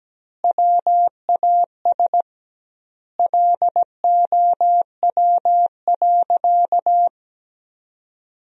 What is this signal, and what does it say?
Morse code, heard as a keyed tone, about 17 wpm, 700 Hz. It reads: WAS LOW.
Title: 2x Word-Spacing - Sentences from Top 500 Words - 17wpm